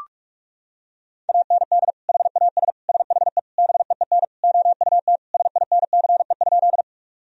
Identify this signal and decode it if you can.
Morse code, heard as a keyed tone, about 46 wpm, 700 Hz. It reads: AND HAS SHE BEEN OUT SINCE?
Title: Sentences from Top 1000 Words - Review Part 1 - 45wpm